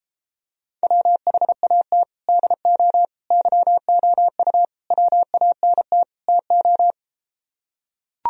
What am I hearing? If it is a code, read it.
Morse code, heard as a keyed tone, about 33 wpm, 700 Hz. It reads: WHAT DO YOU WANT TO